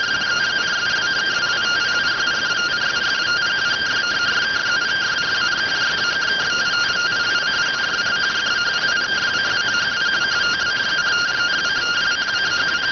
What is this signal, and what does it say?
Two-tone teleprinter signal, about 170 Hz shift.